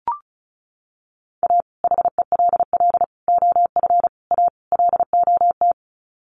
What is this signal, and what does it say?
Morse code, heard as a keyed tone, about 34 wpm, 700 Hz. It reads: A HELL OF A LOT
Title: Sets of 5 Words Encore 35wpm